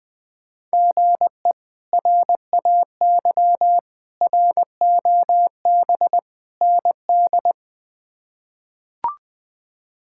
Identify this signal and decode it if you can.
Morse code, heard as a keyed tone, about 20 wpm, 700 Hz. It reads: GE RAY ROB ND
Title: K1USN SST QSO Practice - 20wpm